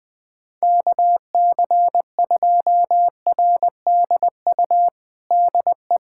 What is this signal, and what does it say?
Morse code, heard as a keyed tone, about 20 wpm, 700 Hz. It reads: KC2RDU DE
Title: USA Callsign DE Callsign 20wpm